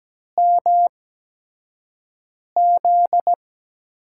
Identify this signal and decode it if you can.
Morse code, heard as a keyed tone, about 17 wpm, spaced as for 5 wpm, 700 Hz. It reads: MZ
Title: USA Call Signs - 8x Character Spacing - 17wpm